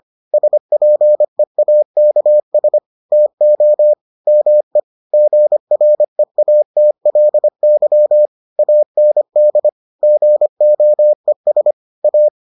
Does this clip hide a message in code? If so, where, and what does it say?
Yes: Morse code from 0.3 to 12.3 s, heard as a keyed tone, about 25 wpm, 600 Hz. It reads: SPEAKS TO ME GREATLY AND GOES A